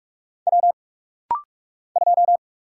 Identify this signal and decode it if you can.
Morse code, heard as a keyed tone, about 46 wpm, 700 Hz. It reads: W 2
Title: Single Letter-Number - Rapid-Fire 45wpm